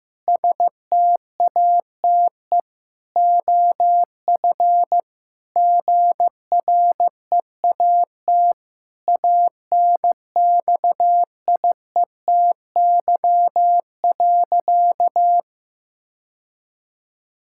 Morse code, heard as a keyed tone, about 15 wpm, 700 Hz. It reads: STATE OF GREAT ANXIETY.
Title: Sentences from Top 1600 Words 15wpm